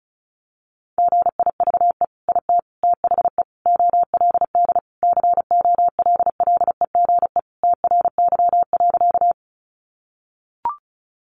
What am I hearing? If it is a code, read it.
Morse code, heard as a keyed tone, about 34 wpm, 700 Hz. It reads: GIVE IT THE OLD COLLEGE TRY.
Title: Sentences from Top 1300 Words 35wpm